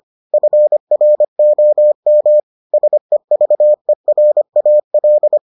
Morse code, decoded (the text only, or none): FROM SEVERAL